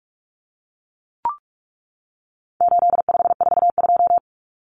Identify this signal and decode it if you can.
Morse code, heard as a keyed tone, about 44 wpm, 700 Hz. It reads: 8542
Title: Common Cut Numbers - 4-Digit 45wpm